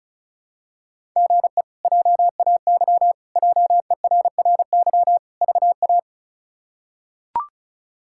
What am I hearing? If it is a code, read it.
Morse code, heard as a keyed tone, about 36 wpm, 700 Hz. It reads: GE JAY JERRY VA
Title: K1USN SST QSO Practice - 35wpm